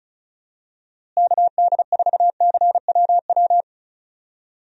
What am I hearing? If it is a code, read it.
Morse code, heard as a keyed tone, about 34 wpm, 700 Hz. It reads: KD4CWW
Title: USA Call Signs - 6 Characters - Repeated 3x - 35wpm